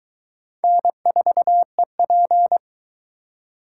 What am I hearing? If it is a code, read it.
Morse code, heard as a keyed tone, about 19 wpm, 700 Hz. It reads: N4EP